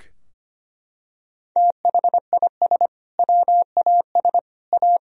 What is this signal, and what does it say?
Morse code, heard as a keyed tone, about 25 wpm, 700 Hz. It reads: THIS WAS A
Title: Sentences from Top 1000 Words - Review Part 2 - 25wpm